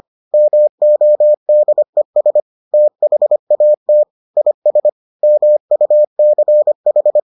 Morse code, heard as a keyed tone, about 25 wpm, 600 Hz. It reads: MODES THAT IS MUCH